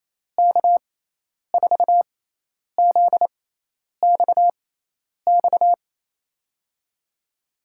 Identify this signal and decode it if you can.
Morse code, heard as a keyed tone, about 28 wpm, spaced as for 10 wpm, 700 Hz. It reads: K4ZXX
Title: USA Call Signs - 6 Characters - 6x Character Spacing - 28wpm